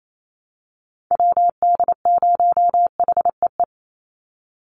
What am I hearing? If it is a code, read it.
Morse code, heard as a keyed tone, about 28 wpm, 700 Hz. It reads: WD0HEE
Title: USA Call Signs 28wpm